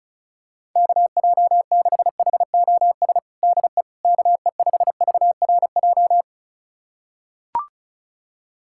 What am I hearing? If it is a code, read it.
Morse code, heard as a keyed tone, about 36 wpm, 700 Hz. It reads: KJ6HOS DE KE5VRJ
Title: USA Callsign DE Callsign - 6 Characters - 35wpm